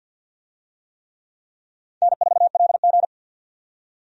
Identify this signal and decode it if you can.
Morse code, heard as a keyed tone, about 52 wpm, 700 Hz. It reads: N4LG
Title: USA Call Signs - 4 Characters - 50wpm